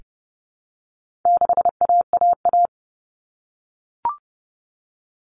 Morse code, heard as a keyed tone, about 30 wpm, 700 Hz. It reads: 6AAA
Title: Common Cut Numbers - 4-Digit 30wpm